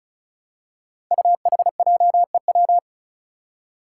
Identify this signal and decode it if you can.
Morse code, heard as a keyed tone, about 34 wpm, 700 Hz. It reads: UHJEW
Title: Five Challenging Letter-Number Combinations 35wpm